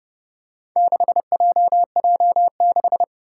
Morse code, heard as a keyed tone, about 30 wpm, 700 Hz. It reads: 6JJ6
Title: Four Challenging Letter-Number Combinations 30wpm